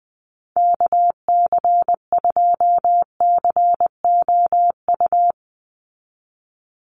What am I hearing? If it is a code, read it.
Morse code, heard as a keyed tone, about 20 wpm, 700 Hz. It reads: KC2COU
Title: USA Call Signs 20wpm